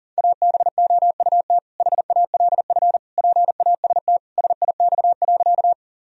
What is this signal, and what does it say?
Morse code, heard as a keyed tone, about 40 wpm, 700 Hz. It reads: ABOUT HALF PAST SIX.
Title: Sentences from Top 1000 Words - Review Part 3 - 40wpm